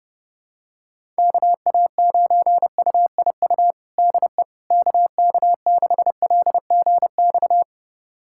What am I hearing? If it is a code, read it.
Morse code, heard as a keyed tone, about 30 wpm, 700 Hz. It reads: KA9UIU DE KK6LGX
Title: USA Callsign DE Callsign - 6 Characters - 30wpm